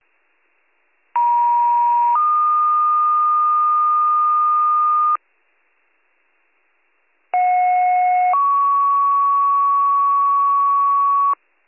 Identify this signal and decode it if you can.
Two-tone page, heard as a one-second tone, then a three-second tone, about 950 then 1220 Hz, about 730 then 1120 Hz. It is heard from 1.2 to 11.3 s.